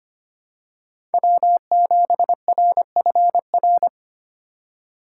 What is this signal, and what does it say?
Morse code, heard as a keyed tone, about 25 wpm, 700 Hz. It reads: W7RFR
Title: USA Call Signs - 5 Characters - 3x Character Spacing - 25wpm